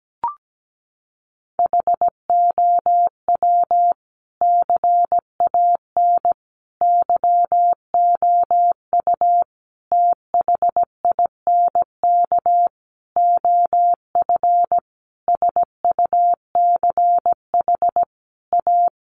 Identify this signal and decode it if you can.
Morse code, heard as a keyed tone, about 17 wpm, 700 Hz. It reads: HOW CAN YOU THINK OF SUCH A
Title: Sentences from Top 200 Words 17wpm